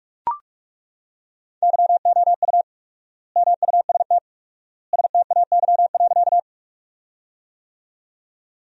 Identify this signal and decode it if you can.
Morse code, heard as a keyed tone, about 41 wpm, 700 Hz. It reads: YOU MUST STAY.
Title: 4x Word-Spacing - Sentences from Top 500 Words - 45wpm